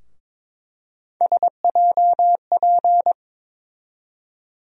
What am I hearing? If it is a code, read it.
Morse code, heard as a keyed tone, about 22 wpm, 700 Hz. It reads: SJP
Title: USA Call Sign Postfixes - 3 Letters 22wpm